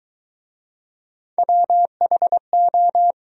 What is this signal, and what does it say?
Morse code, heard as a keyed tone, about 23 wpm, 700 Hz. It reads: WHO